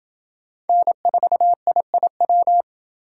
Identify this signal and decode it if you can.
Morse code, heard as a keyed tone, about 28 wpm, 700 Hz. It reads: N4IIW